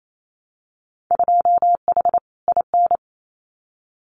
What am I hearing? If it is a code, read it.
Morse code, heard as a keyed tone, about 28 wpm, 690 Hz. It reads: 2H IN
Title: Winter Field Day Exchanges Encore 28wpm